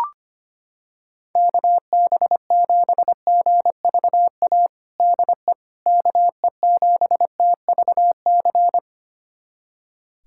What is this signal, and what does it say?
Morse code, heard as a keyed tone, about 25 wpm, 700 Hz. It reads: KB7GVA DE KE7TVC
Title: USA Callsign DE Callsign - 6 Characters - 25wpm